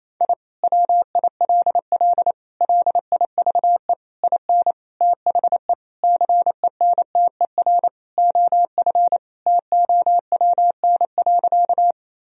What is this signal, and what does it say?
Morse code, heard as a keyed tone, about 28 wpm, 700 Hz. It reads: I WILL LIVE IN THE CENTER OF TOWN.
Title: Sentences from Top 1000 Words - Review Part 2 - 28wpm